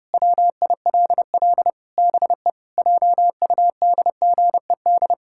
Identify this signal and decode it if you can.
Morse code, heard as a keyed tone, about 30 wpm, 700 Hz. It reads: WILL BE JUDGED
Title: Pangram Challenge v2 - 30wpm